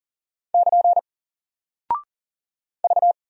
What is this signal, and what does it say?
Morse code, heard as a keyed tone, about 40 wpm, 700 Hz. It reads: <KN> V
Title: Single Character (Letter:Number:Punctuation:Prosign) 40wpm